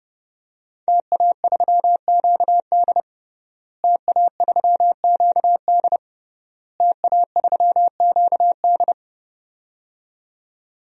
Morse code, heard as a keyed tone, about 30 wpm, 700 Hz. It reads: TA3QD TA3QD TA3QD
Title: DXCC-100 Call Signs - 5 Characters - Repeated 3x - 30wpm